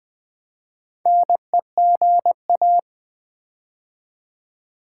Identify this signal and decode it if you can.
Morse code, heard as a keyed tone, about 20 wpm, 700 Hz. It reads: NEGA
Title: Build A Word - 4 Letters At A Time - 8-Letter Words - 20wpm